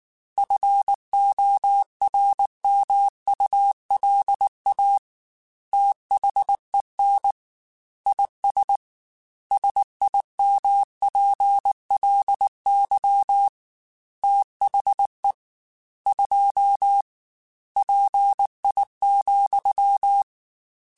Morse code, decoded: FORMULA THEN IS SIMPLY THE 2 PI,

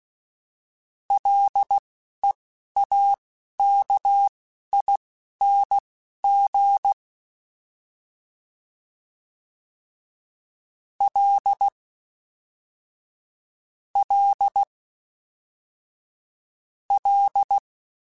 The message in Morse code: LEAKING L L L

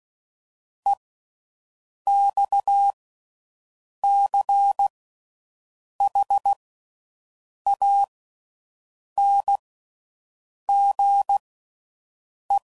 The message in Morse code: EXCHANGE

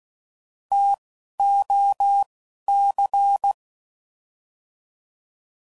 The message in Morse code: T O C

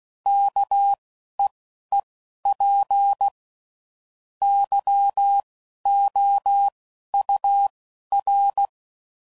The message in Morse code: KEEP YOUR

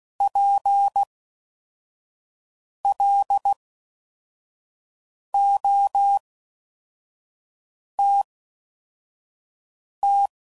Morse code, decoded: PLOTT

